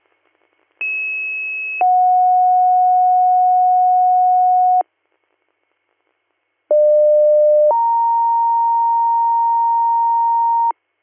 Two-tone page, heard as a one-second tone, then a three-second tone, about 2600 then 730 Hz, about 600 then 930 Hz. It is heard from 0.8 to 10.7 s.
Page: bradford-pa